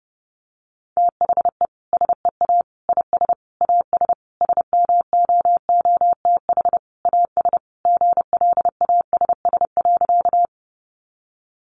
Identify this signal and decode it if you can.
Morse code, heard as a keyed tone, about 30 wpm, 700 Hz. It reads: THE SEA IS AS SMOOTH AS GLASS.
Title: Sentences from Top 1400 Words 30wpm